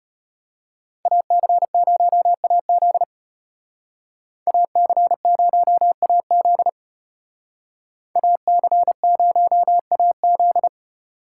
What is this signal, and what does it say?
Morse code, heard as a keyed tone, about 33 wpm, 700 Hz. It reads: AC0AZ AC0AZ AC0AZ